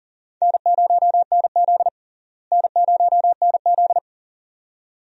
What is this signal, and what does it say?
Morse code, heard as a keyed tone, about 38 wpm, 700 Hz. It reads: N0NZ N0NZ
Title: USA Call Signs - 4 Characters - Repeated 2x - 40wpm